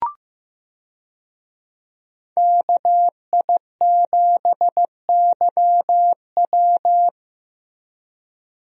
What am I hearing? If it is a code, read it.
Morse code, heard as a keyed tone, about 15 wpm, 700 Hz. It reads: KI7YW
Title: USA Call Signs - 5 Characters - Repeated 3x - 15wpm